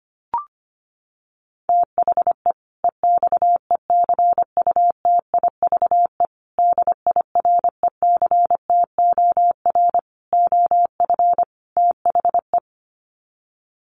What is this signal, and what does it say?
Morse code, heard as a keyed tone, about 25 wpm, 700 Hz. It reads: THE EXECUTIVE DIRECTOR OF THE
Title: Sets of 5 Words Encore 25wpm